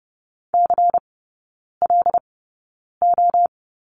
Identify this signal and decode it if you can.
Morse code, heard as a keyed tone, about 30 wpm, spaced as for 9 wpm, 700 Hz. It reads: CLO
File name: Lesson 16e - Letter c - Three Characters 30wpm